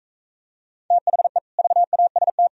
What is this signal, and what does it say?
Morse code, heard as a keyed tone, about 45 wpm, 700 Hz. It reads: THE VAST